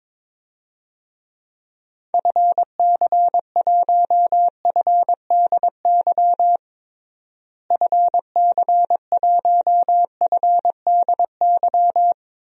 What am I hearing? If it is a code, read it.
Morse code, heard as a keyed tone, about 23 wpm, 700 Hz. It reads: FC1FDY FC1FDY
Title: DXCC-100 Call Signs - 6 Characters - Repeated 3x - 22wpm